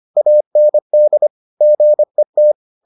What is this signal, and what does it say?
Morse code, heard as a keyed tone, about 25 wpm, 600 Hz. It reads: AND GET